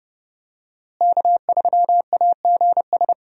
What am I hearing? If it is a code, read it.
Morse code, heard as a keyed tone, about 30 wpm, 700 Hz. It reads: K3AGS
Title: USA Call Signs - 5 Characters - Repeated 3x - 30wpm